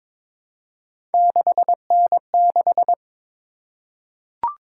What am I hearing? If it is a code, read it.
Morse code, heard as a keyed tone, about 22 wpm, 700 Hz. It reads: 6N6
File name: practice / 22wpm / Three Letter-Number Combinations - 22wpm